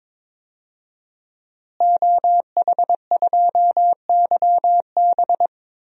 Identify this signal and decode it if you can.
Morse code, heard as a keyed tone, about 22 wpm, 700 Hz. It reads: OH2YB
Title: DXCC-100 Call Signs - 5 Characters - Repeated 2x - 22wpm